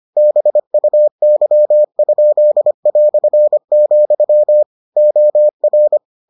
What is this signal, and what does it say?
Morse code, heard as a keyed tone, about 25 wpm, 600 Hz. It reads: BUY?", OR